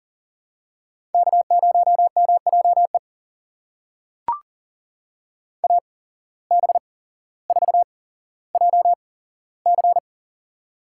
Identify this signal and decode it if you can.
Morse code, heard as a keyed tone, about 41 wpm, spaced as for 17 wpm, 700 Hz. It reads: K0MJE AB4JC
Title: USA Call Signs - 5 Characters - 8x Character Spacing - 40wpm